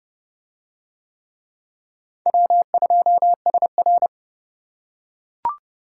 Morse code, heard as a keyed tone, about 29 wpm, 700 Hz. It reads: W2SR